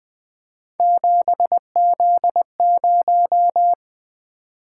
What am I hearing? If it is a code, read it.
Morse code, heard as a keyed tone, about 20 wpm, 700 Hz. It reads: 7Z0
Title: Three Letter-Number Combinations - 20wpm